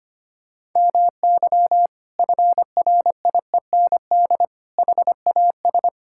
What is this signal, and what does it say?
Morse code, heard as a keyed tone, about 25 wpm, 700 Hz. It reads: MY FRIEND HAS
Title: Sentences from Top 1600 Words 25wpm